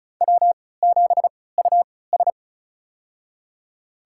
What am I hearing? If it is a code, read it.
Morse code, heard as a keyed tone, about 34 wpm, spaced as for 20 wpm, 700 Hz. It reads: W7US
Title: USA Call Signs - 4 Characters - 3x Character Spacing - 35wpm